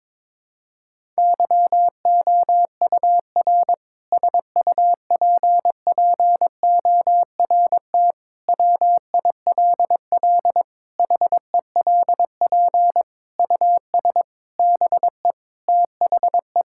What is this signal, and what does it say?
Morse code, heard as a keyed tone, about 23 wpm, 700 Hz. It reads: YOUR SUPPORT WILL HELP US BE THE
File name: Sentences with 7-Word Phrases - 22wpm